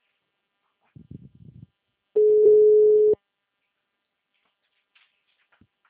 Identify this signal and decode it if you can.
Ringback tone (Europe, 425 Hz).
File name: out- -2008